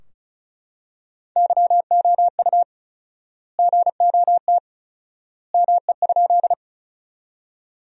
Morse code, heard as a keyed tone, about 34 wpm, 700 Hz. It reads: YOU GOT ME?
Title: 4x Word-Spacing - Sentences from Top 300 Words - 35wpm